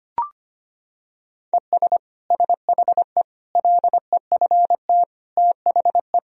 Morse code, heard as a keyed tone, about 25 wpm, 700 Hz. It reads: ES SHE LEFT THE
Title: Sentences from Top 300 Words and Abbreviations 25wpm